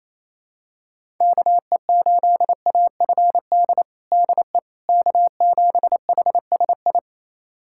Morse code, heard as a keyed tone, about 27 wpm, 700 Hz. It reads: KE8AFD DE K7HSI
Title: USA Callsign DE Callsign 28wpm